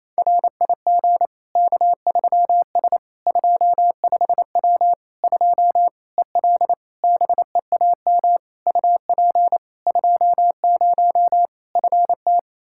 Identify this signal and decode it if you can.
Morse code, heard as a keyed tone, about 28 wpm, 700 Hz. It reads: RIG K3S 25W 2 EL BEAM UP 20 FT
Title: Practice QSO - Exchange 2 - Info - 28wpm